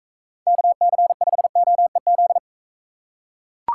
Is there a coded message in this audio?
Yes: Morse code from 0.5 to 2.4 s, heard as a keyed tone, about 44 wpm, 700 Hz. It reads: KC5OEZ